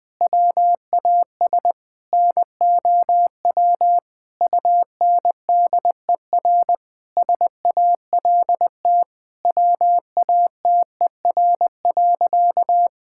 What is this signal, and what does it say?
Morse code, heard as a keyed tone, about 20 wpm, 700 Hz. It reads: WAS NOW UNDER SALT WATER.
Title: Sentences from Top 1000 Words - Review Part 5 - 20wpm